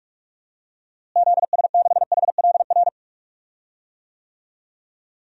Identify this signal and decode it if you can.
Morse code, heard as a keyed tone, about 46 wpm, 700 Hz. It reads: ZS6HLR
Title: DXCC-100 Call Signs - 6 Characters - Repeated 3x - 45wpm